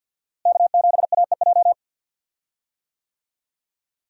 Morse code, heard as a keyed tone, about 48 wpm, 700 Hz. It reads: K7AEJ
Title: USA Call Signs - 5 Characters - Repeated 2x - 50wpm